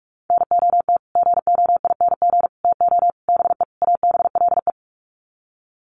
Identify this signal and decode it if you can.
Morse code, heard as a keyed tone, about 44 wpm, 700 Hz. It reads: NOT GOING TO BE ABLE
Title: Sets of 5 Words 45wpm